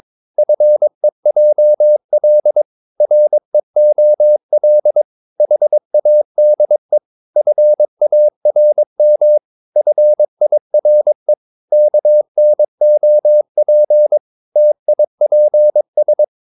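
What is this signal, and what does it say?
Morse code, heard as a keyed tone, about 22 wpm, 600 Hz. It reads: FEJL REOL HADE FARM FIRE KNOP TIPS
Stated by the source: Morse practice of short Danish words at 22 wpm, from CW med Gnister.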